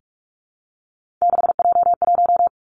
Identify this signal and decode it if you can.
Morse code, heard as a keyed tone, about 46 wpm, 700 Hz. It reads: 6J1
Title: Three Challenging Letter-Number Combinations 45wpm